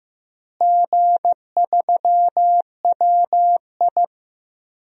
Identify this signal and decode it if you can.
Morse code, heard as a keyed tone, about 15 wpm, 700 Hz. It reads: G3WI
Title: DXCC-100 Call Signs - 4 Characters - Repeated 2x - 15wpm